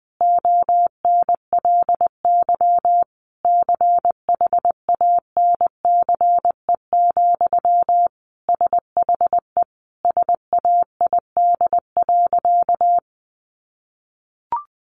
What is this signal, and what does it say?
Morse code, heard as a keyed tone, about 20 wpm, 700 Hz. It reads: ONLY CHANCE, SHE SAID.
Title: Sentences from Top 800 Words 20wpm